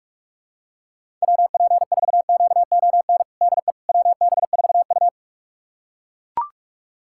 Morse code, heard as a keyed tone, about 42 wpm, 700 Hz. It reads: WP4QON DE WB4U
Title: USA Callsign DE Callsign 45wpm